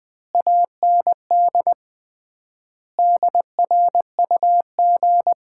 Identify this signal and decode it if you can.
Morse code, heard as a keyed tone, about 21 wpm, 700 Hz. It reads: AND DRUG
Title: 3x Word-Spacing - Sets of 5 Words - 20wpm